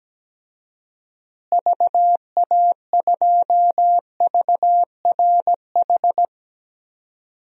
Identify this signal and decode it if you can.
Morse code, heard as a keyed tone, about 17 wpm, 700 Hz. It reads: VA2VRH
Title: DXCC-100 Call Signs - 6 Characters - Repeated 3x - 17wpm